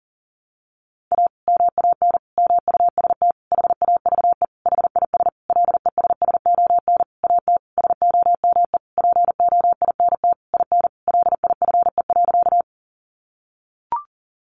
Morse code, heard as a keyed tone, about 40 wpm, 700 Hz. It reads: A MAN MUST HAVE HIS LESSON AT SOME POINT IN LIFE.
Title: Sentences from Top 1200 Words 40wpm